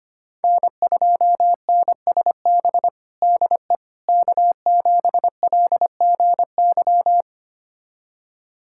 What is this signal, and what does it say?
Morse code, heard as a keyed tone, about 25 wpm, 700 Hz. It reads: N2NSB DE K7LGY
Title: USA Callsign DE Callsign - 5 Characters - 25wpm